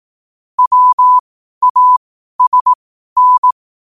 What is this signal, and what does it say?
Morse code, heard as a keyed tone, about 18 wpm, spaced as for 12 wpm, 1000 Hz. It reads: WASN
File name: qotdfiles / 12wpm